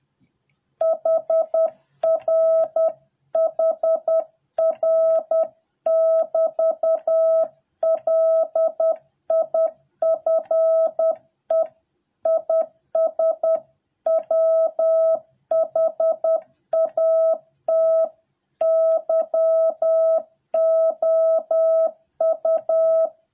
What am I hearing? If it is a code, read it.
Morse code, heard as a keyed tone, about 10 wpm, 650 Hz. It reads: HR HR <BT> LIFE IS WHAT YOU